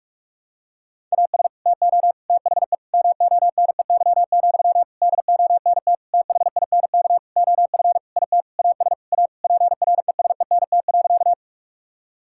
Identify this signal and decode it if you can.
Morse code, heard as a keyed tone, about 46 wpm, 700 Hz. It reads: AS TO THE MONEY, DONT THINK OF IT AS A PRESENT.
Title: Sentences from Top 1000 Words - Review Part 5 - 45wpm